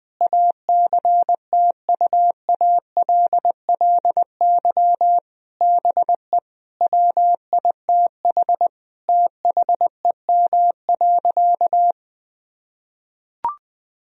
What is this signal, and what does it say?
Morse code, heard as a keyed tone, about 20 wpm, 700 Hz. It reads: ACTUALLY BE WITH THEM.
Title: Sentences from Top 1000 Words - Review Part 4 - 20wpm